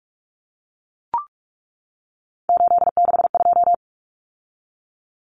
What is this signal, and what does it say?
Morse code, heard as a keyed tone, about 44 wpm, 700 Hz. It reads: 862
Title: Common Cut Numbers - 3-Digit 45wpm